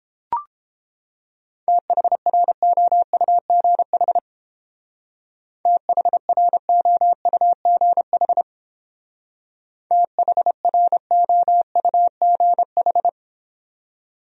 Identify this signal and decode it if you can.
Morse code, heard as a keyed tone, about 31 wpm, 700 Hz. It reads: THROUGH THROUGH THROUGH